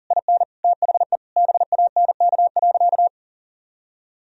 Morse code, heard as a keyed tone, about 40 wpm, 700 Hz. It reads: IN THE BANK.
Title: Sentences from Top 1000 Words - Review Part 4 - 40wpm